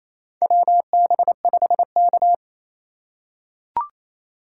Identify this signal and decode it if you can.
Morse code, heard as a keyed tone, about 28 wpm, 700 Hz. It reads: WB5K
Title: K1USN SST QSO Practice - 28wpm